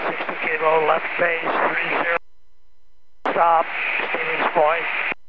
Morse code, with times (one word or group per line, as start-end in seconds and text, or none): none